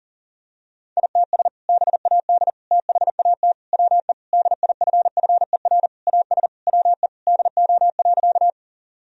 Morse code, heard as a keyed tone, about 40 wpm, 700 Hz. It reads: ITS BAD THAT WE DIFFER AS WE DO.
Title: Sentences from Top 1000 Words - Review Part 1 - 40wpm